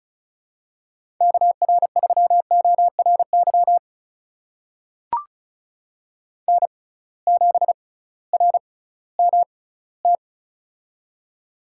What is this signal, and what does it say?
Morse code, heard as a keyed tone, about 36 wpm, spaced as for 19 wpm, 700 Hz. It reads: KR3ORY N7RMT
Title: USA Call Signs - 6x Character Spacing - 35wpm